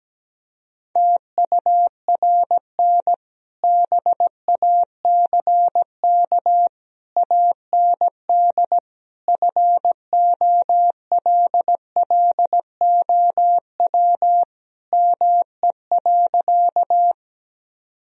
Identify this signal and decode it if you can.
Morse code, heard as a keyed tone, about 17 wpm, 700 Hz. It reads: TURN BACK AND FOLLOW ME.
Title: Sentences from Top 1000 Words - Review Part 1 - 17wpm